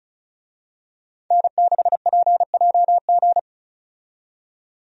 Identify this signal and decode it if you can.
Morse code, heard as a keyed tone, about 36 wpm, 700 Hz. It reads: N6PJG
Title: USA Call Signs - 5 Characters - 35wpm